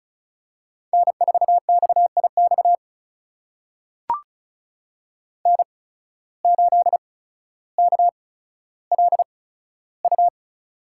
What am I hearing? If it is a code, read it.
Morse code, heard as a keyed tone, about 36 wpm, spaced as for 15 wpm, 690 Hz. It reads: N4XIX N8KLU
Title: USA Call Signs - 5 Characters - 8x Character Spacing - 35wpm